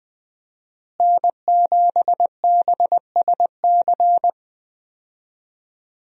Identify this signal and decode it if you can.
Morse code, heard as a keyed tone, about 20 wpm, 700 Hz. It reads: N7BSC
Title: USA Call Signs - 5 Characters - Repeated 2x - 20wpm